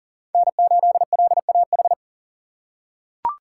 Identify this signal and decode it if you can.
Morse code, heard as a keyed tone, about 40 wpm, 700 Hz. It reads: N8LAH
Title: USA Call Signs - 5 Characters - 2x Character Spacing - 40wpm